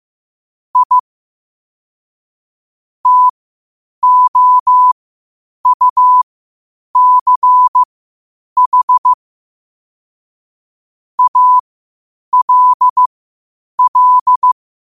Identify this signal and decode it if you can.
Morse code, heard as a keyed tone, about 15 wpm, spaced as for 8 wpm, 1000 Hz. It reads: I TOUCH ALL
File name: qotdfiles / 8wpm